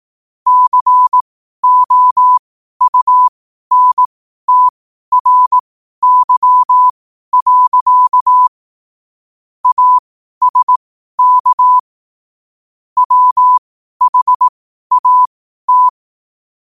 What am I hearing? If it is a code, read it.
Morse code, heard as a keyed tone, about 18 wpm, spaced as for 12 wpm, 1000 Hz. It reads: COUNTRY. ASK WHAT
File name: qotdfiles / 12wpm